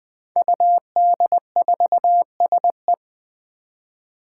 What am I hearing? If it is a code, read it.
Morse code, heard as a keyed tone, about 20 wpm, 700 Hz. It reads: UD4SE
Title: Five Challenging Letter-Number Combinations 20wpm